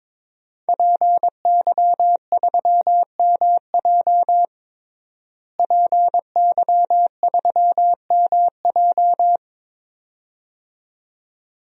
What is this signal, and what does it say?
Morse code, heard as a keyed tone, about 22 wpm, 700 Hz. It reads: PY3MJ PY3MJ
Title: DXCC-100 Call Signs - 5 Characters - Repeated 2x - 22wpm